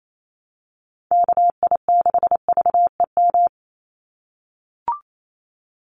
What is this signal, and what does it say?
Morse code, heard as a keyed tone, about 28 wpm, 700 Hz. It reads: KI6VEM
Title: USA Call Signs 28wpm